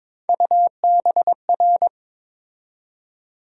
Morse code, heard as a keyed tone, about 21 wpm, 700 Hz. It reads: UBR
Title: USA Call Sign Postfixes - 3 Letters 22wpm